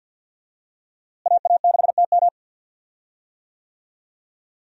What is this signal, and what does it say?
Morse code, heard as a keyed tone, about 52 wpm, 700 Hz. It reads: AA6TM
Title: USA Call Signs - 5 Characters - Repeated 3x - 50wpm